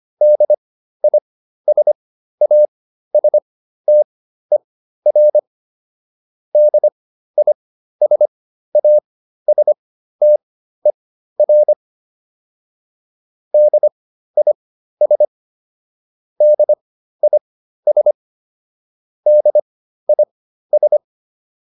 Morse code, heard as a keyed tone, about 25 wpm, spaced as for 13 wpm, 600 Hz. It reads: DISASTER DISASTER DIS DIS DIS